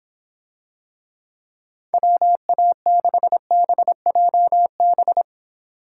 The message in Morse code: WA6BJB

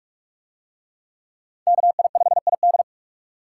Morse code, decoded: KI5ID

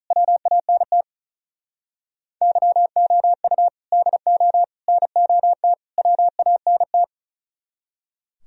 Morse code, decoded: WANT YOU DO NOT WANT

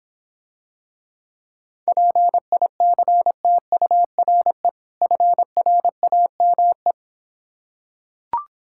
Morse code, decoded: PICTURE FRAME

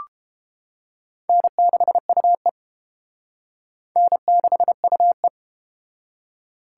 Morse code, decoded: N6UE N6UE

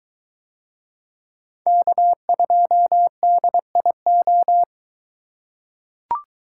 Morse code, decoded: K2DIO

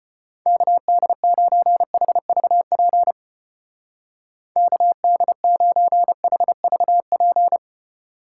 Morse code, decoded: KD9HVP KD9HVP

